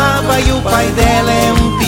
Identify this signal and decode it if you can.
There is no code.